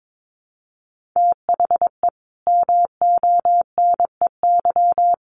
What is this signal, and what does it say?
Morse code, heard as a keyed tone, about 22 wpm, 700 Hz. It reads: THE MONEY
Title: Sentences from Top 1400 Words 22wpm